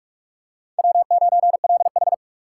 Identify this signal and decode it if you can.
Morse code, heard as a keyed tone, about 41 wpm, 700 Hz. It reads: W9LH
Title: USA Call Signs - 4 Characters - 4x Character Spacing - 45wpm